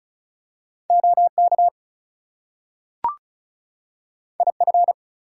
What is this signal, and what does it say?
Morse code, heard as a keyed tone, about 36 wpm, 700 Hz. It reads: OK IF
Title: Two Letter Words - 35wpm